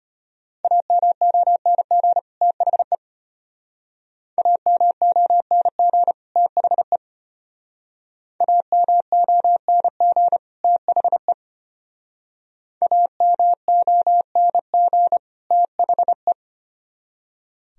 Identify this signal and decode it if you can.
Morse code, heard as a keyed tone, about 33 wpm, 700 Hz. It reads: AMONG THE AMONG THE AMONG THE AMONG THE